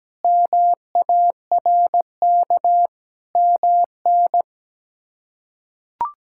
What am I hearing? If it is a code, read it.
Morse code, heard as a keyed tone, about 17 wpm, 700 Hz. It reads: MARK MN
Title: K1USN SST QSO Practice - 17wpm